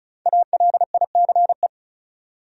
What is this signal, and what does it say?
Morse code, heard as a keyed tone, about 36 wpm, 700 Hz. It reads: ALICE
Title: USA Common Names - Max 5 Letters - 35wpm